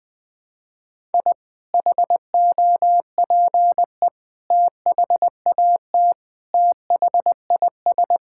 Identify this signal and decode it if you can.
Morse code, heard as a keyed tone, about 20 wpm, 700 Hz. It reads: I HOPE THAT THIS
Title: Sentences with 2-Word Phrases - 20wpm